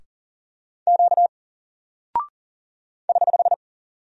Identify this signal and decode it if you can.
Morse code, heard as a keyed tone, about 40 wpm, 700 Hz. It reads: Q <HH>
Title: Single Character (Letter:Number:Punctuation:Prosign) Alt 40wpm